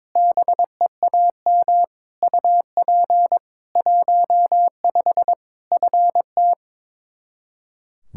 Morse code, heard as a keyed tone, about 22 wpm, 700 Hz. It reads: BEAM UP 15 FT
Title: Practice QSO - Exchange 2 - Info - 22wpm